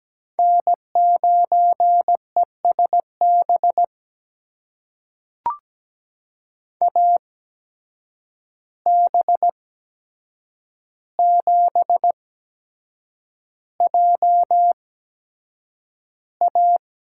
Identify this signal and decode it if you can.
Morse code, heard as a keyed tone, about 17 wpm, spaced as for 7 wpm, 700 Hz. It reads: N9ESB AB7JA
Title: USA Call Signs - 5 Characters - 8x Character Spacing - 17wpm